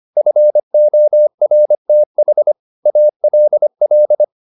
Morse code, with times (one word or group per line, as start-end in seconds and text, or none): 0.2-2.5 FORTH
2.9-4.2 ALL